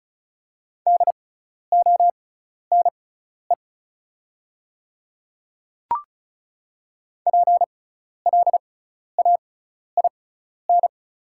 Morse code, heard as a keyed tone, about 36 wpm, spaced as for 12 wpm, 690 Hz. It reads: DONE PLAIN
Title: 6x Character Spacing - Top 500 Words - 35wpm